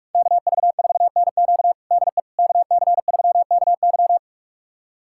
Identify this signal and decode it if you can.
Morse code, heard as a keyed tone, about 45 wpm, 700 Hz. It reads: KV4NQ DE KC3KY
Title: USA Callsign DE Callsign 45wpm